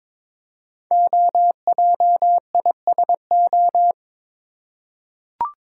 Morse code, heard as a keyed tone, about 22 wpm, 700 Hz. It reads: OJISO
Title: Code Groups - 5 Letters - 22wpm